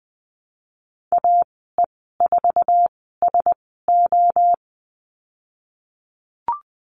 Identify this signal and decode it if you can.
Morse code, heard as a keyed tone, about 20 wpm, spaced as for 15 wpm, 700 Hz. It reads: AE4SO
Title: 088 Lesson 15g - Number 5 - Call Signs 20wpm